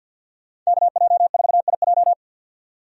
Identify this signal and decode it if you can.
Morse code, heard as a keyed tone, about 52 wpm, 700 Hz. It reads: KJ4IJ
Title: USA Call Signs - 5 Characters - 50wpm